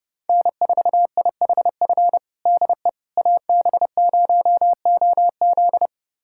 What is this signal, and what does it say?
Morse code, heard as a keyed tone, about 32 wpm, 700 Hz. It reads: N4IHF DE AB0OZ
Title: USA Callsign DE Callsign - 5 Characters - 30wpm